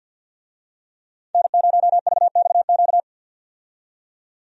Morse code, heard as a keyed tone, about 53 wpm, 700 Hz. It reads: N0VXQ
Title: USA Call Signs - 5 Characters - 50wpm